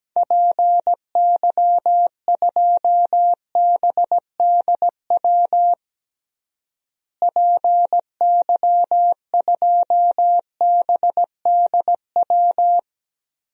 Morse code, heard as a keyed tone, about 17 wpm, 700 Hz. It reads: PY2BDW PY2BDW
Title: DXCC-100 Call Signs - 6 Characters - Repeated 3x - 17wpm